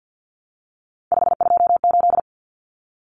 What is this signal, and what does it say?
Morse code, heard as a keyed tone, about 50 wpm, 700 Hz. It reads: 528
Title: Common Cut Numbers - 3-Digit 50wpm